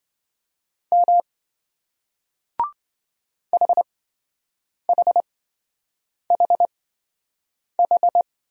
Morse code, heard as a keyed tone, about 27 wpm, spaced as for 7 wpm, 700 Hz. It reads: M HHHH